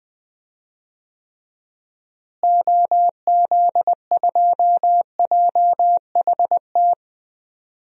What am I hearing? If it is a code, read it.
Morse code, heard as a keyed tone, about 20 wpm, 700 Hz. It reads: OZ2JHT
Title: DXCC-100 Call Signs - 6 Characters - Repeated 2x - 20wpm